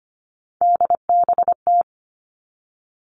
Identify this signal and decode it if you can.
Morse code, heard as a keyed tone, about 25 wpm, 700 Hz. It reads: DBT
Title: Three Challenging Letter-Number Combinations 25wpm